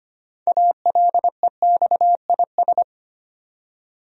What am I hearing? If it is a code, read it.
Morse code, heard as a keyed tone, about 25 wpm, 700 Hz. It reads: ALEXIS
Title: USA Common Names - 25wpm